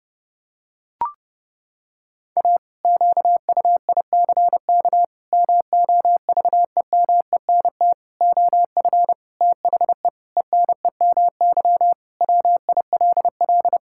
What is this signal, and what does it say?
Morse code, heard as a keyed tone, about 32 wpm, 700 Hz. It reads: A QUICK MOVEMENT OF THE ENEMY WILL
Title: Pangram Challenge v2 - 30wpm